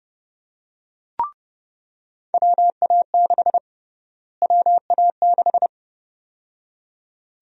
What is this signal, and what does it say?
Morse code, heard as a keyed tone, about 30 wpm, 700 Hz. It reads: WA6 WA6
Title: USA Call Sign Prefixes - Repeated 2x - 30wpm